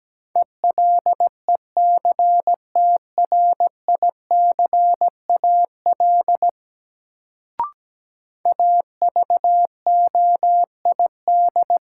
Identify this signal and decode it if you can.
Morse code, heard as a keyed tone, about 17 wpm, 700 Hz. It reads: ELECTRICAL AVOID
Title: Sentences from Top 1000 Words - Review Part 6 - New Word Drill - 17wpm